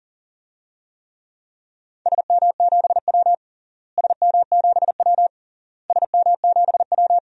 Morse code, heard as a keyed tone, about 37 wpm, 700 Hz. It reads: SM7W SM7W SM7W
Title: DXCC-100 Call Signs - 4 Characters - Repeated 3x - 40wpm